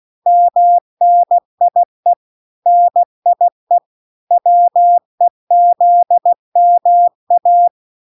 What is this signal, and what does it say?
Morse code, heard as a keyed tone, about 16 wpm, 700 Hz. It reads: MNIE NIE WEZMA